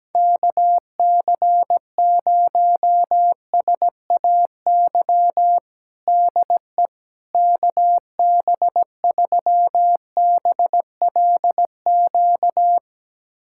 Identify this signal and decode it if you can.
Morse code, heard as a keyed tone, about 17 wpm, 700 Hz. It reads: KC0SAY DE KB3BLQ
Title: USA Callsign DE Callsign - 6 Characters - 17wpm